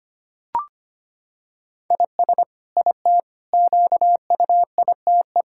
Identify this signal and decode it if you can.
Morse code, heard as a keyed tone, about 25 wpm, 700 Hz. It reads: IS IT QUITE
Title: Sentences from Top 1000 Words - Review Part 6 - 25wpm